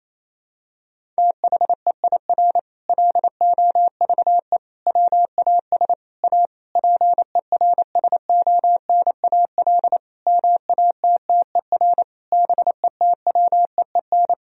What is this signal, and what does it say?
Morse code, heard as a keyed tone, about 28 wpm, 700 Hz. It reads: THEIR LOVE WAS A PERSONAL MATTER BETWEEN
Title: Sentences from Top 1000 Words - Review Part 4 - 28wpm